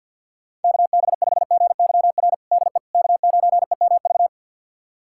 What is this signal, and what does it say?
Morse code, heard as a keyed tone, about 52 wpm, 700 Hz. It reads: KB5GYR DE K9EMV
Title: USA Callsign DE Callsign 50wpm